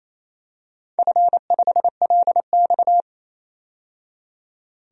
Morse code, heard as a keyed tone, about 28 wpm, 700 Hz. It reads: F5LX
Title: DXCC-100 Call Signs - 4 Characters - 28wpm